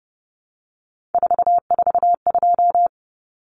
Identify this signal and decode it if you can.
Morse code, heard as a keyed tone, about 30 wpm, 700 Hz. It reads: 442
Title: Numbers - 3-Digit 30wpm